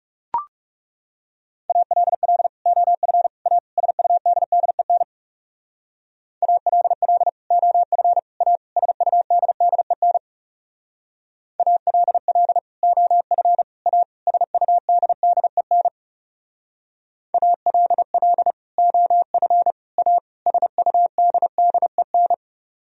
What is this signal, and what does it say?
Morse code, heard as a keyed tone, about 35 wpm, 700 Hz. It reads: ALL OF A SUDDEN ALL OF A SUDDEN ALL OF A SUDDEN ALL OF A SUDDEN
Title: Speed-Racing - Sets of 4 Words 30wpm